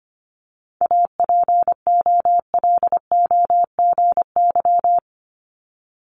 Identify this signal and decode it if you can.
Morse code, heard as a keyed tone, about 25 wpm, 700 Hz. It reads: APOLOGY
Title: Multi-letter Phonemes Repeated 3x 25wpm